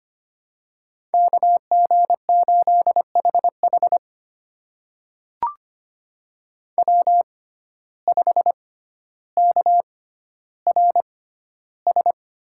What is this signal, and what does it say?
Morse code, heard as a keyed tone, about 25 wpm, spaced as for 13 wpm, 700 Hz. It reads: KG8HH W5KRS
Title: USA Call Signs - 5 Characters - 6x Character Spacing - 25wpm